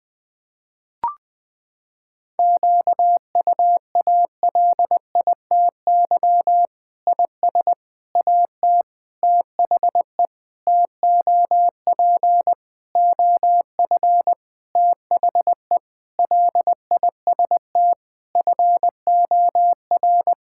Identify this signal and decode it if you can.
Morse code, heard as a keyed tone, about 20 wpm, 700 Hz. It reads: QUALITY IS AT THE TOP OF THE LIST FOR
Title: Sentences with 6-Word Phrases - 20wpm